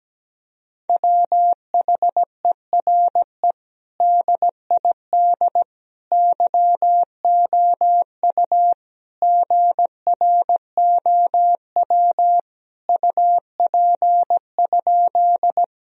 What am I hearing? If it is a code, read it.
Morse code, heard as a keyed tone, about 17 wpm, 700 Hz. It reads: WHERE DID YOU GROW UP?
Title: Sentences from Top 1000 Words - Review Part 1 - 17wpm